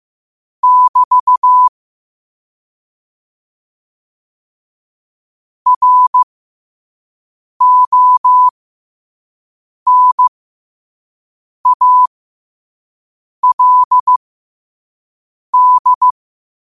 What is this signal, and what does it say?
Morse code, heard as a keyed tone, about 15 wpm, spaced as for 5 wpm, 1000 Hz. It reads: <BT> RONALD
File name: qotdfiles / 5wpm